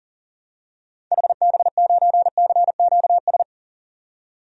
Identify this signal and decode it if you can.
Morse code, heard as a keyed tone, about 40 wpm, 700 Hz. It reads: HB9CQS
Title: DXCC-100 Call Signs - 6 Characters - 40wpm